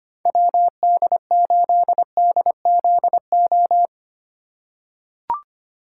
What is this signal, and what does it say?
Morse code, heard as a keyed tone, about 25 wpm, 700 Hz. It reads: WD8DZO
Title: USA Call Signs - 2x Character Spacing - 25wpm